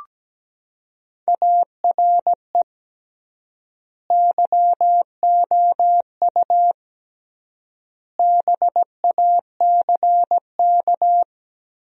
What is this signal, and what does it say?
Morse code, heard as a keyed tone, about 17 wpm, 700 Hz. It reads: ARE YOU BACK
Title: 3x Word-Spacing - Sentences from Top 200 Words - 17wpm